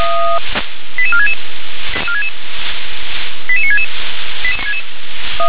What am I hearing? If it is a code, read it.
Five-tone selective call (ZVEI), heard as quick runs of five tones, tones of about 70 ms. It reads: 88466 88966